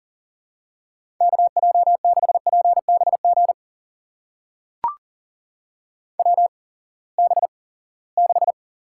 Morse code, heard as a keyed tone, about 40 wpm, spaced as for 23 wpm, 700 Hz. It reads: KJ6PBG WB6
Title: USA Call Signs - 8x Character Spacing - 40wpm